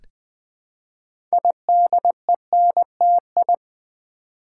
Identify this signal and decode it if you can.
Morse code, heard as a keyed tone, about 20 wpm, 700 Hz. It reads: IDENTI
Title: Build A Word - 2 Letters At A Time - 8-Letter Words - 20wpm